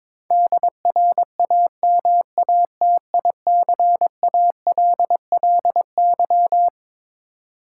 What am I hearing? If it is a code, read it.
Morse code, heard as a keyed tone, about 22 wpm, 700 Hz. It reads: DRAMATICALLY
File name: Twelve Letter Words - 22wpm